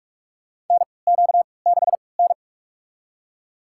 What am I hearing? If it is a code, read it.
Morse code, heard as a keyed tone, about 42 wpm, spaced as for 25 wpm, 700 Hz. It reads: NQ6N
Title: USA Call Signs - 3x Character Spacing - 45wpm